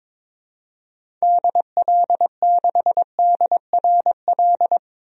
Morse code, heard as a keyed tone, about 22 wpm, 700 Hz. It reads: DL6DRL
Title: DXCC-100 Call Signs - 6 Characters - 22wpm